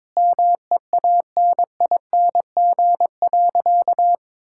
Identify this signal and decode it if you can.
Morse code, heard as a keyed tone, about 22 wpm, 700 Hz. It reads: MEANING.
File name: practice / 22wpm / Sentences with 2-Word Phrases - 22wpm